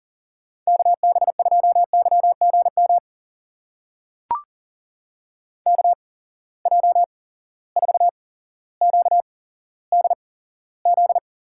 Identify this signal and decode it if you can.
Morse code, heard as a keyed tone, about 40 wpm, spaced as for 17 wpm, 700 Hz. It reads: KB2YGM KJ4QDZ
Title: USA Call Signs - 6 Characters - 8x Character Spacing - 40wpm